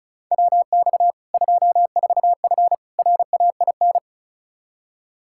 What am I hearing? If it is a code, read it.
Morse code, heard as a keyed tone, about 36 wpm, 700 Hz. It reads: WX 24F RAIN
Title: Practice QSO - Exchange 2 - Info - 35wpm